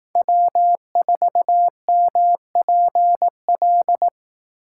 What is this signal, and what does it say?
Morse code, heard as a keyed tone, about 18 wpm, 700 Hz. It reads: W4MPL
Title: Speed Racing - USA 1x3 Call Signs 15wpm